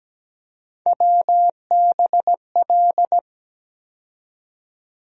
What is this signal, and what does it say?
Morse code, heard as a keyed tone, about 17 wpm, 700 Hz. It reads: WBL
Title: USA Call Sign Postfixes - 3 Letters 17wpm